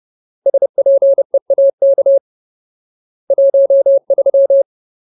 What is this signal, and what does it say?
Morse code, heard as a keyed tone, about 30 wpm, 550 Hz. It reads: SPEAK 13
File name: Ultimate Word List / 30WPM Ultimate